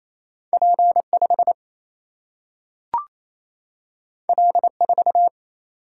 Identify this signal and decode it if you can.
Morse code, heard as a keyed tone, about 28 wpm, 700 Hz. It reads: P5 L4
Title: Single-Letter Single-Number 28wpm